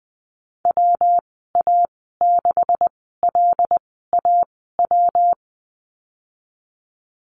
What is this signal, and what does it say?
Morse code, heard as a keyed tone, about 20 wpm, spaced as for 14 wpm, 700 Hz. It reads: WA6LAW
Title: 161 Lesson 26g - Letter g - Call Signs 20wpm